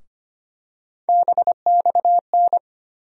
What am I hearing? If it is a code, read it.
Morse code, heard as a keyed tone, about 25 wpm, 700 Hz. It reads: BXN
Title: USA Call Sign Postfixes - 3 Letters 25wpm